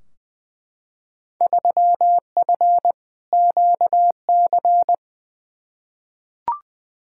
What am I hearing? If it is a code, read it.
Morse code, heard as a keyed tone, about 20 wpm, 700 Hz. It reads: 3F QC
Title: Field Day Exchanges - 20wpm